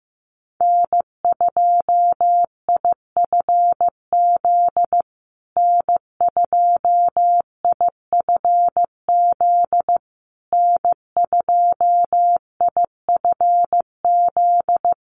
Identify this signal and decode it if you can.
Morse code, heard as a keyed tone, about 15 wpm, 700 Hz. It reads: N2IFZ N2IFZ N2IFZ
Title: USA Call Signs Repeated 3x 15wpm